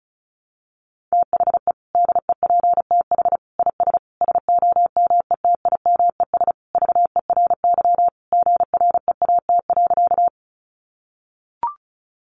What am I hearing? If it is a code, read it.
Morse code, heard as a keyed tone, about 34 wpm, 700 Hz. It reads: THE DEPTH IS SOMETIMES VERY GREAT.
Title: Sentences from Top 1200 Words 35wpm